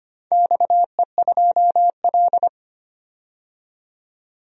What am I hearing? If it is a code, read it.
Morse code, heard as a keyed tone, about 25 wpm, 700 Hz. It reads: XE2L
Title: DXCC-100 Call Signs - 4 Characters - 25wpm